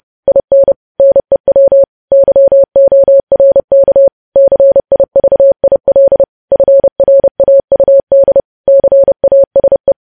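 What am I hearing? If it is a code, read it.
Morse code, heard as a keyed tone, about 30 wpm, 550 Hz. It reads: IN NEW YORK CIVIL FRAUD CASE
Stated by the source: News headlines sent in Morse code at 30.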